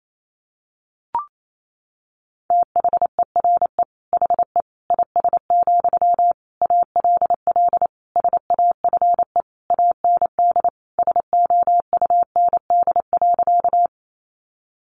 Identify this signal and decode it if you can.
Morse code, heard as a keyed tone, about 28 wpm, 690 Hz. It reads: THERE HE IS, ALL SAFE AND SOUND.
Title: Sentences from Top 900 Words 28wpm